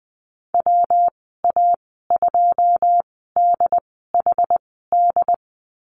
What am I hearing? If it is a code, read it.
Morse code, heard as a keyed tone, about 20 wpm, spaced as for 14 wpm, 700 Hz. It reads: WA2DHD
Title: 154 Lesson 25g - Letter p - Call Signs 20wpm